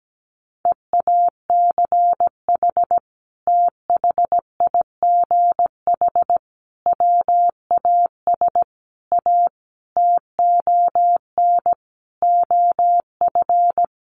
Morse code, heard as a keyed tone, about 17 wpm, 700 Hz. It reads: EACH THIGH WAS A TON OF
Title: A Sound Of Thunder 17wpm